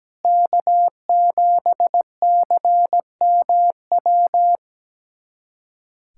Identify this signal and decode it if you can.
Morse code, heard as a keyed tone, about 17 wpm, 700 Hz. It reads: K7CMW
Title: USA Callsign DE Callsign 17wpm